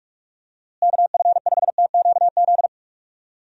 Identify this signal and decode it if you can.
Morse code, heard as a keyed tone, about 41 wpm, 700 Hz. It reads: KF5TQZ
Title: USA Call Signs - 6 Characters - Repeated 2x - 45wpm